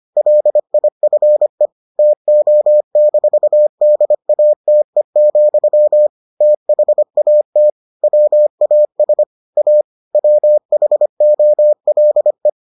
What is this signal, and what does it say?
Morse code, heard as a keyed tone, about 25 wpm, 600 Hz. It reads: LIFE TO-DATE, THAT WAS A WHOLE